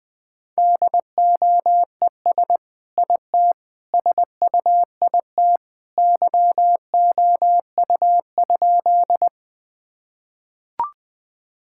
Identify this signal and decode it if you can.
Morse code, heard as a keyed tone, about 20 wpm, 700 Hz. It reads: DOES IT SUIT YOU?
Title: Sentences from Top 1600 Words 20wpm